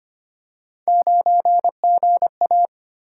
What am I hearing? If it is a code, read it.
Morse code, heard as a keyed tone, about 25 wpm, 700 Hz. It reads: 9GA
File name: Three Letter-Number Combinations - 25wpm